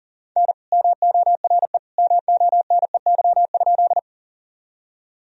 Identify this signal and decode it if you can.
Morse code, heard as a keyed tone, about 37 wpm, 700 Hz. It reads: N MORE MONEY?
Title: Sentences from Top 400 Words and Abbreviations 40wpm